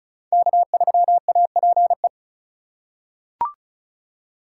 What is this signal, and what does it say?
Morse code, heard as a keyed tone, about 36 wpm, 700 Hz. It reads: K3APE
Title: USA Call Signs - 5 Characters - 2x Character Spacing - 35wpm